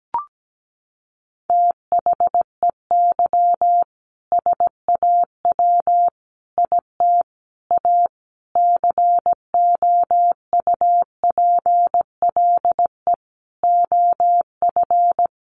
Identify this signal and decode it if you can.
Morse code, heard as a keyed tone, about 17 wpm, 700 Hz. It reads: THEY SAW IT A COUPLE OF
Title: Sentences from Top 1100 Words 17wpm